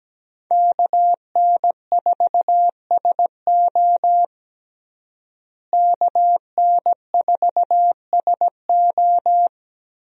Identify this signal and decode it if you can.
Morse code, heard as a keyed tone, about 17 wpm, 700 Hz. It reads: KN4SO KN4SO
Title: USA Call Signs - 5 Characters - Repeated 3x - 17wpm